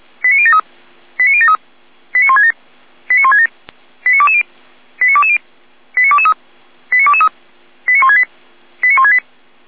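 Five-tone selective call (ZVEI), heard as quick runs of five tones, tones of about 80 ms. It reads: 89073 89073 89167 89167 89220 89220 89223 89223 89168 89168
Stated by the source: Five-tone ZVEI selective call received by SDR.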